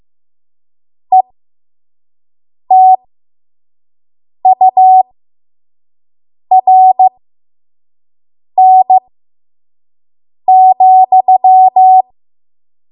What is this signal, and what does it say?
Morse code, heard as a keyed tone, about 15 wpm, spaced as for 5 wpm, 750 Hz. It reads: ETURN,
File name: Morse / 05WPM